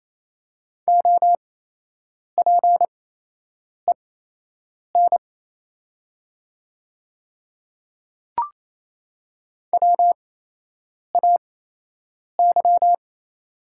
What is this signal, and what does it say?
Morse code, heard as a keyed tone, about 28 wpm, spaced as for 8 wpm, 700 Hz. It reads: OPEN WAY
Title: 8x Character Spacing - Top 500 Words - 28wpm